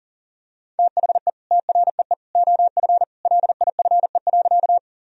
Morse code, heard as a keyed tone, about 40 wpm, 700 Hz. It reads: THE TREE OF LIFE.
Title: Sentences from Top 1000 Words - Review Part 1 - 40wpm